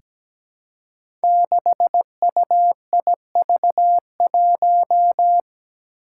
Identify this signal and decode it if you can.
Morse code, heard as a keyed tone, about 17 wpm, 700 Hz. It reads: 6UIV1
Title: Code Groups - 5 Letter:Numbers - 17wpm